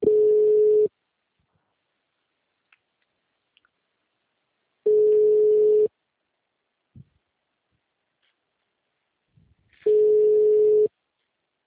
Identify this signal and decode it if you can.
Ringback tone (Europe, 425 Hz).